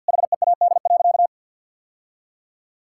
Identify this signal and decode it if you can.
Morse code, heard as a keyed tone, about 51 wpm, 700 Hz. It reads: HEAD.
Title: Sentences from Top 1000 Words - Review Part 6 - 50wpm